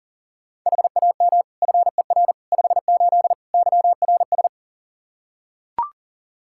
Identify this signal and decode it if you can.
Morse code, heard as a keyed tone, about 40 wpm, 700 Hz. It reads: HAM FER 58 YRS
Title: K1USN SST QSO Practice 40wpm